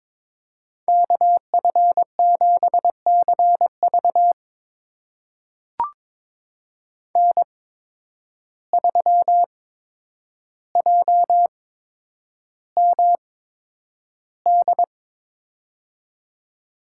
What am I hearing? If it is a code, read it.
Morse code, heard as a keyed tone, about 23 wpm, spaced as for 10 wpm, 700 Hz. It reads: KF7CV N3JMD
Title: USA Call Signs - 5 Characters - 8x Character Spacing - 22wpm